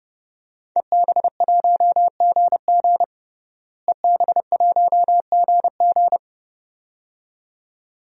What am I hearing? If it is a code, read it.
Morse code, heard as a keyed tone, about 32 wpm, 700 Hz. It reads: EB1GG EB1GG
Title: DXCC-100 Call Signs - 5 Characters - Repeated 2x - 30wpm